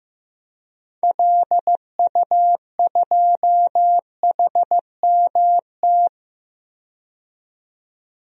Morse code, heard as a keyed tone, about 15 wpm, 690 Hz. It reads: LU2HMT
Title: DXCC-100 Call Signs - 6 Characters - 15wpm